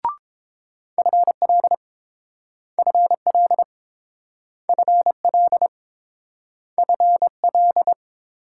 Morse code, heard as a keyed tone, about 30 wpm, 700 Hz. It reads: FL FL FL FL